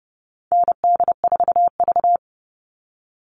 Morse code, heard as a keyed tone, about 30 wpm, 700 Hz. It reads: ND4V
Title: Call Signs Repeated 2x - 30wpm